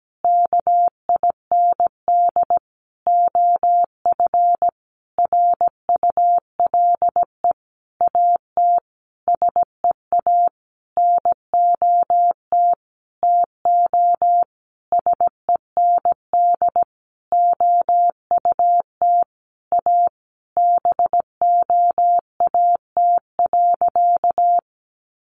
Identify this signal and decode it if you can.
Morse code, heard as a keyed tone, about 17 wpm, 700 Hz. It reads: KIND OF RULE AT SEA NOT TO SEND OUT A BOAT.
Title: Sentences from Top 1500 Words 17wpm